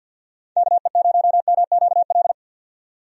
Morse code, heard as a keyed tone, about 47 wpm, 700 Hz. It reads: KE0MQL
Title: USA Call Signs - 6 Characters - 50wpm